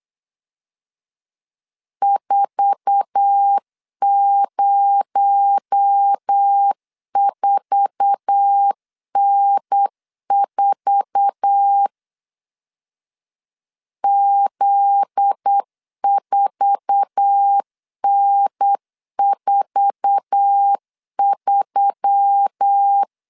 Morse code, heard as a keyed tone, about 8 wpm, 800 Hz. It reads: 404N4 Z4N43
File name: LEZIONI / 8 YJ45 / 8 YJ4 8wpm 3246